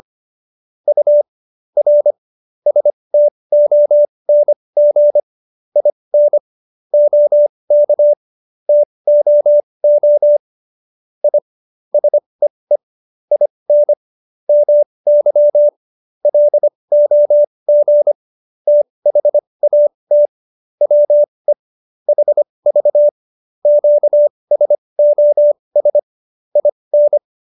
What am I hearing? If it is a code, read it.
Morse code, heard as a keyed tone, about 25 wpm, spaced as for 20 wpm, 590 Hz. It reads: U R STONG IN OK TOO I SEE IN MY LOG THAT WE HV QSOS IN
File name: QSO316 20